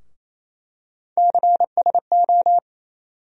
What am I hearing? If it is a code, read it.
Morse code, heard as a keyed tone, about 28 wpm, 700 Hz. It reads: CSO
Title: USA Call Sign Postfixes - 3 Letters 28wpm